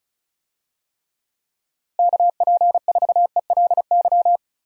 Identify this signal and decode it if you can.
Morse code, heard as a keyed tone, about 34 wpm, 700 Hz. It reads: KP4ELY
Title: USA Call Signs - 6 Characters - Repeated 3x - 35wpm